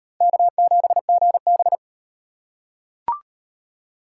Morse code, heard as a keyed tone, about 34 wpm, 700 Hz. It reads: K7GB